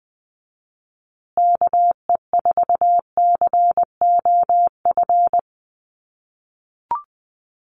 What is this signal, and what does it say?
Morse code, heard as a keyed tone, about 20 wpm, 700 Hz. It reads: KE4COF
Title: USA Call Signs Repeated 3x 20wpm